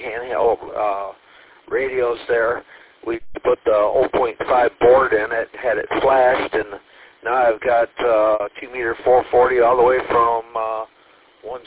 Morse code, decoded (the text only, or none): none